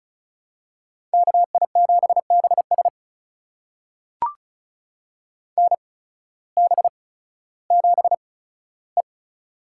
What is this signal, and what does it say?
Morse code, heard as a keyed tone, about 36 wpm, spaced as for 16 wpm, 700 Hz. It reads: KI7BS NB7E
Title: USA Call Signs - 8x Character Spacing - 35wpm